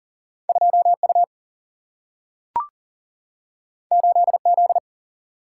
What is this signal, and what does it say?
Morse code, heard as a keyed tone, about 39 wpm, 700 Hz. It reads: 2 U 8 Z
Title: Single-Letter Single-Number 40wpm